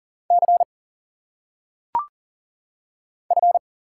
Morse code, heard as a keyed tone, about 40 wpm, 700 Hz. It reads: C F
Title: Single Letters - Phonetic Pronunciation 40wpm